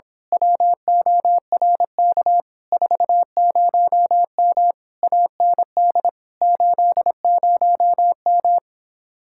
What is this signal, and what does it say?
Morse code, heard as a keyed tone, about 26 wpm, 700 Hz. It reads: WORK 40M AND 80M
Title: Speed-Racing Common QSO Phrases - 22wpm